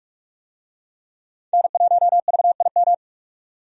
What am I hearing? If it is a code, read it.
Morse code, heard as a keyed tone, about 41 wpm, 700 Hz. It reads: N1VIM
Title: USA Call Signs - 5 Characters - 45wpm